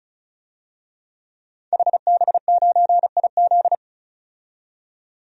Morse code, heard as a keyed tone, about 36 wpm, 700 Hz. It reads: HB9IZ
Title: DXCC-100 Call Signs - 5 Characters - 35wpm